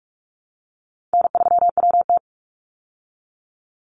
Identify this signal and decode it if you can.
Morse code, heard as a keyed tone, about 44 wpm, 700 Hz. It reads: N3WT
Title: Easy Call Signs Repeated 3x - 45wpm